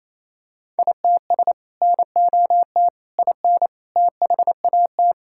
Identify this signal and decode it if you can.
Morse code, heard as a keyed tone, about 28 wpm, 700 Hz. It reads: ITS NOT IN THAT
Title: Sentences from Top 1000 Words - Review Part 5 - 28wpm